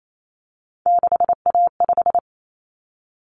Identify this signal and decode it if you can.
Morse code, heard as a keyed tone, about 28 wpm, 700 Hz. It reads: 6A5
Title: Common Cut Numbers - 3-Digit 28wpm